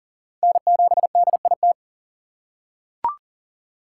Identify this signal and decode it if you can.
Morse code, heard as a keyed tone, about 37 wpm, 700 Hz. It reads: N7DIT
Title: USA Call Signs - 5 Characters - 4x Character Spacing - 40wpm